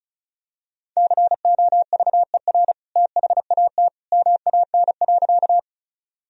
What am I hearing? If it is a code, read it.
Morse code, heard as a keyed tone, about 36 wpm, 700 Hz. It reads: COVER THAT MAN.
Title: Sentences from Top 1000 Words - Review Part 1 - 35wpm